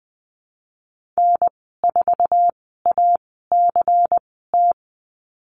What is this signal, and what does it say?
Morse code, heard as a keyed tone, about 20 wpm, spaced as for 15 wpm, 700 Hz. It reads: N4ACT
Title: 167 Lesson 27g - Number 7 - Call Signs 20wpm